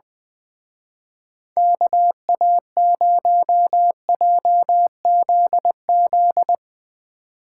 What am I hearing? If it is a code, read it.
Morse code, heard as a keyed tone, about 21 wpm, 700 Hz. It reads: KA0JZZ
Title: Speed Racing - USA Call Signs 20wpm